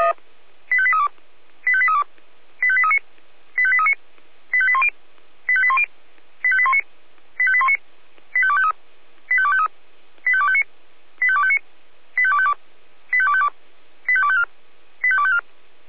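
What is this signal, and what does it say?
Five-tone selective call (ZVEI), heard as quick runs of five tones, tones of about 70 ms. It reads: 85832 85832 85839 85839 85810 85810 85819 85819 85253 85253 85259 85259 85252 85252 85254 85254
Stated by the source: ZVEI five-tone selective call received on an SDR.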